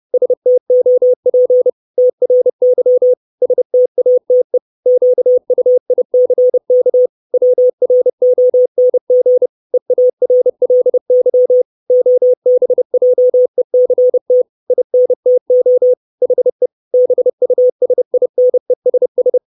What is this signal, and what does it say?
Morse code, heard as a keyed tone, about 30 wpm, 500 Hz. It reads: STOP TRY STATE QUICK WRONG EARLY OBJECT INTO HE BUSINESS
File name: 632 Words 30WPM